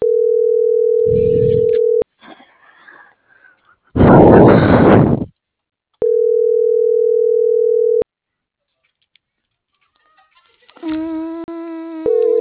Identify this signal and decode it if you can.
Ringback tone (North America), 440 + 480 Hz.